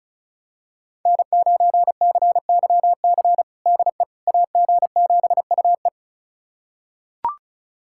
Morse code, heard as a keyed tone, about 36 wpm, 700 Hz. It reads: N9CYC DE AG7UE
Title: USA Callsign DE Callsign - 5 Characters - 35wpm